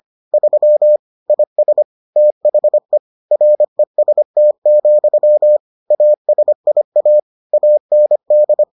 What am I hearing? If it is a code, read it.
Morse code, heard as a keyed tone, about 25 wpm, 600 Hz. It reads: 3 IS THE REST, ASIA AND